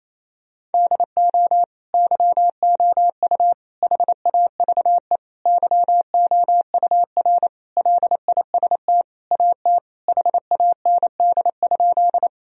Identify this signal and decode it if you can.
Morse code, heard as a keyed tone, about 28 wpm, 700 Hz. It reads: DO YOU HAVE YOUR LIST AT HAND?
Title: Sentences from Top 1000 Words - Review Part 2 - 28wpm